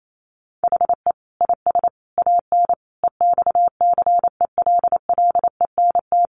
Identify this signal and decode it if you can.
Morse code, heard as a keyed tone, about 28 wpm, 700 Hz. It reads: HE IS AN EXCELLENT
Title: Sentences from Top 1000 Words 28wpm